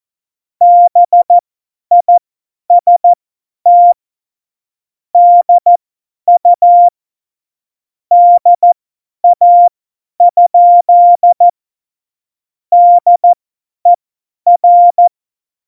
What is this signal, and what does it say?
Morse code, heard as a keyed tone, about 14 wpm, spaced as for 10 wpm, 700 Hz. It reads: BIST DU DA? DER
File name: Morsekurs / Witze / Witz 15. Dezember 2020, 50 F70